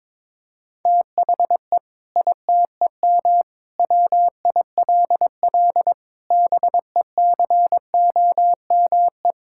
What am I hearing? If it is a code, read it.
Morse code, heard as a keyed tone, about 22 wpm, 700 Hz. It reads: THE ITEM WILL BECOME
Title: Sentences from Top 1000 Words - Review Part 6 - 22wpm